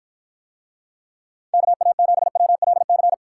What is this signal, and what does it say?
Morse code, heard as a keyed tone, about 52 wpm, 700 Hz. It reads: KA7WLC